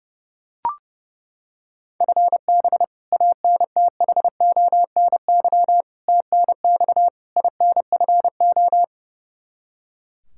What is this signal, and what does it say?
Morse code, heard as a keyed tone, about 32 wpm, 700 Hz. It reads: FB ANTHONY TNX INFO
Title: K1USN SST QSO Practice 30wpm